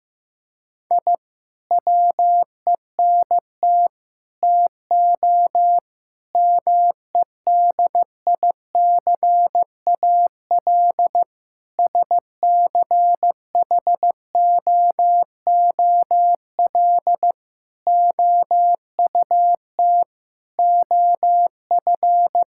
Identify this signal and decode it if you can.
Morse code, heard as a keyed tone, about 15 wpm, 700 Hz. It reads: I WENT TO MEDICAL SCHOOL OUT OF
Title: Sentences from Top 1000 Words - Review Part 5 - 15wpm